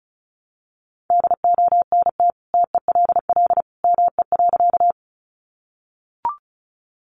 Morse code, heard as a keyed tone, about 34 wpm, 700 Hz. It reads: DONT TELL ME.
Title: Sentences from Top 300 Words 35wpm